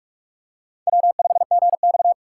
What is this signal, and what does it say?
Morse code, heard as a keyed tone, about 47 wpm, 700 Hz. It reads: W5GX
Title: USA Call Signs - 4 Characters - 4x Character Spacing - 45wpm